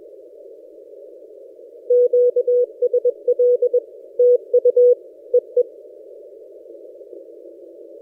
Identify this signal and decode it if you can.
Morse code, heard as a keyed tone, about 21 wpm, 490 Hz. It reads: QSL TU EE
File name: posts / a35jr-12cw- -2336z